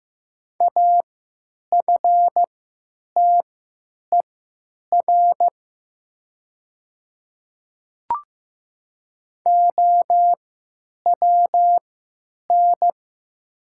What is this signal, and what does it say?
Morse code, heard as a keyed tone, about 15 wpm, spaced as for 9 wpm, 700 Hz. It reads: AFTER OWN
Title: 3x Character Spacing - Top 300 Words - 15wpm